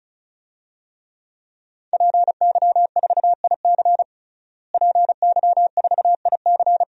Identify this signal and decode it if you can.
Morse code, heard as a keyed tone, about 36 wpm, 700 Hz. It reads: PY4IC PY4IC
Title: DXCC-100 Call Signs - 5 Characters - Repeated 2x - 35wpm